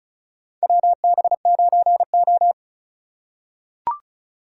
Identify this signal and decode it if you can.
Morse code, heard as a keyed tone, about 36 wpm, 700 Hz. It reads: WB9O
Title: USA Call Signs - 4 Characters - 3x Character Spacing - 35wpm